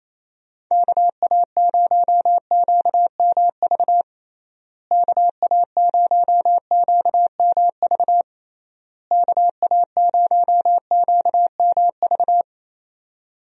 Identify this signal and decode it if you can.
Morse code, heard as a keyed tone, about 28 wpm, 700 Hz. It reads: KA0QMV KA0QMV KA0QMV
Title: USA Call Signs - 6 Characters - Repeated 3x - 28wpm